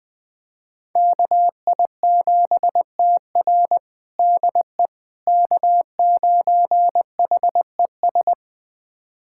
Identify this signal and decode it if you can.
Morse code, heard as a keyed tone, about 21 wpm, 700 Hz. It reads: KI7TR DE K9HES
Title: USA Callsign DE Callsign - 5 Characters - 20wpm